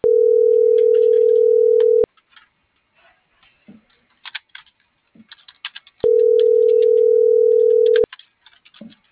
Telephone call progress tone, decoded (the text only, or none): ringback tone